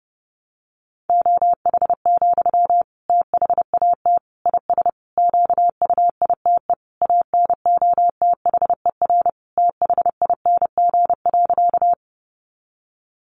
Morse code, heard as a keyed tone, about 30 wpm, 700 Hz. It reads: OH, THAT IS QUITE ANOTHER THING.
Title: Sentences from Top 700 Words 30wpm